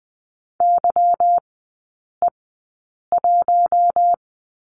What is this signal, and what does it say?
Morse code, heard as a keyed tone, about 20 wpm, spaced as for 8 wpm, 700 Hz. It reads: YE1
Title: 145 Lesson 24e - Letter y - Three Characters 20wpm